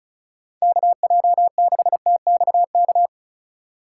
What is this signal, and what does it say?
Morse code, heard as a keyed tone, about 36 wpm, 700 Hz. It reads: KJ6TXK
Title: USA Call Signs - 6 Characters - 6x Character Spacing - 35wpm